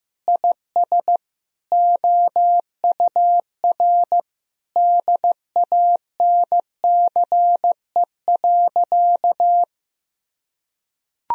Morse code, heard as a keyed tone, about 16 wpm, 700 Hz. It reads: IS OUR DANCE.
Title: Sentences from Top 1000 Words - Review Part 4 - 15wpm